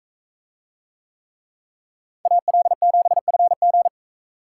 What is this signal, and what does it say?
Morse code, heard as a keyed tone, about 45 wpm, 700 Hz. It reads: AL7FG